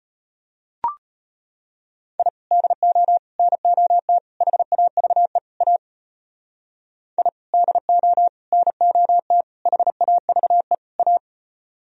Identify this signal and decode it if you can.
Morse code, heard as a keyed tone, about 31 wpm, 700 Hz. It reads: I DO NOT HAVE A I DO NOT HAVE A
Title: Speed-Racing - Sets of 5 Words 25wpm